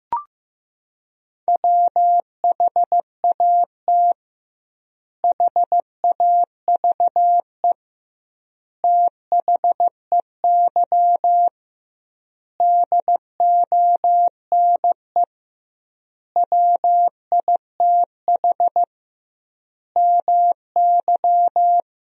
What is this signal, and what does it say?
Morse code, heard as a keyed tone, about 15 wpm, 700 Hz. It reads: WHAT HAVE THEY DONE WITH MY
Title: 2x Word-Spacing - Sentences from Top 500 Words - 15wpm